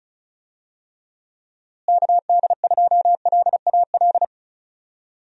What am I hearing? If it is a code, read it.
Morse code, heard as a keyed tone, about 34 wpm, 700 Hz. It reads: KD2LAL
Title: USA Call Signs - 6 Characters - Repeated 2x - 35wpm